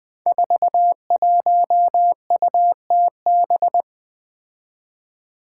Morse code, heard as a keyed tone, about 20 wpm, 700 Hz. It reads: 41UTB
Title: Five Challenging Letter-Number Combinations 20wpm